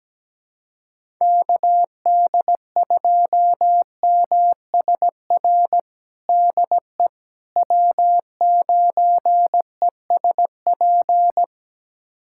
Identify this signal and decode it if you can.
Morse code, heard as a keyed tone, about 17 wpm, 700 Hz. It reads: KD2MSR DE W9ESP
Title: USA Callsign DE Callsign 17wpm